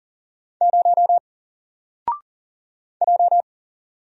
Morse code, heard as a keyed tone, about 39 wpm, 700 Hz. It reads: 0 J
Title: Single Letter-Number 40wpm